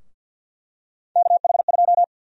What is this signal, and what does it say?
Morse code, heard as a keyed tone, about 49 wpm, 700 Hz. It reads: KH2
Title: USA Call Sign Prefixes - 50wpm